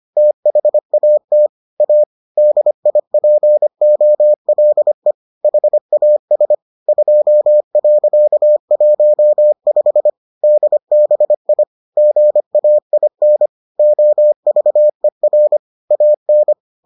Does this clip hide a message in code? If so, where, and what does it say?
Yes: Morse code from 0.2 to 16.5 s, heard as a keyed tone, about 25 wpm, 600 Hz. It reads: THAT A DIPOLE HAS 2.15 DBI GAIN OVER AN